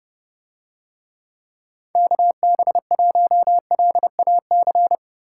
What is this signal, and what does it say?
Morse code, heard as a keyed tone, about 28 wpm, 700 Hz. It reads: KB1LAC